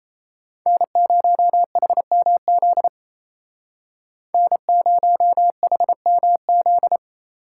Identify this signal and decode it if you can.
Morse code, heard as a keyed tone, about 33 wpm, 700 Hz. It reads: N0HMZ N0HMZ
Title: Speed Racing - USA 1x3 Call Signs 28wpm